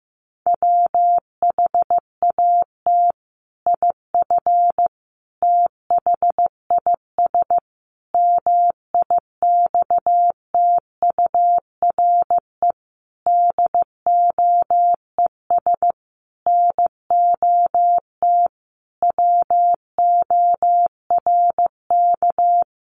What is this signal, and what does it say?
Morse code, heard as a keyed tone, about 15 wpm, 700 Hz. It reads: WHAT IF THIS MIXTURE DOES NOT WORK
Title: Sentences from Top 1500 Words 15wpm